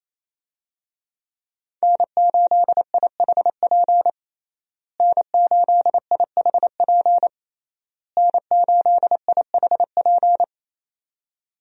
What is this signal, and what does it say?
Morse code, heard as a keyed tone, about 28 wpm, 700 Hz. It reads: N8IHP N8IHP N8IHP
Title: USA Call Signs - 5 Characters - Repeated 3x - 28wpm